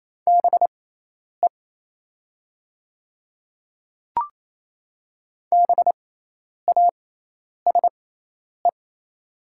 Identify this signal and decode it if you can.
Morse code, heard as a keyed tone, about 28 wpm, spaced as for 10 wpm, 700 Hz. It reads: BE BASE
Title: 6x Character Spacing - Top 300 Words - 28wpm